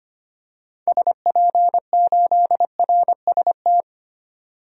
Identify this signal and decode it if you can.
Morse code, heard as a keyed tone, about 25 wpm, 700 Hz. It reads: SP8RST
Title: DXCC-100 Call Signs - 6 Characters - 25wpm